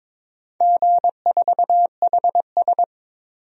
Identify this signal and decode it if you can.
Morse code, heard as a keyed tone, about 22 wpm, 700 Hz. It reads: G4HS
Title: DXCC-100 Call Signs - 4 Characters - 22wpm